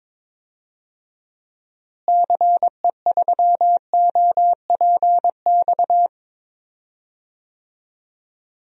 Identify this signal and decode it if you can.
Morse code, heard as a keyed tone, about 22 wpm, 700 Hz. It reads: CE3OPX
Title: DXCC-100 Call Signs - 6 Characters - Repeated 2x - 22wpm